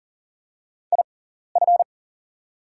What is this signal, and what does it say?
Morse code, heard as a keyed tone, about 40 wpm, spaced as for 14 wpm, 690 Hz. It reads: IF
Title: 6x Character Spacing - Top 300 Words - 40wpm